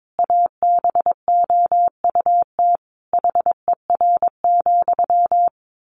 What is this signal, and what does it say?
Morse code, heard as a keyed tone, about 22 wpm, 700 Hz. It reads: ABOUT HER,
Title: Sentences from Top 1500 Words 22wpm